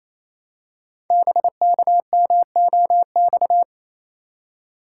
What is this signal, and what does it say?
Morse code, heard as a keyed tone, about 28 wpm, 700 Hz. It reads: BKMOX
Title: Code Groups - 5 Letters - 28wpm